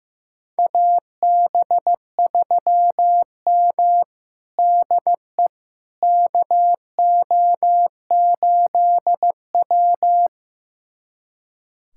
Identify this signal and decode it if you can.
Morse code, heard as a keyed tone, about 15 wpm, 700 Hz. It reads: AB3M DE KO8W
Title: USA Callsign DE Callsign - 4 Characters - 15wpm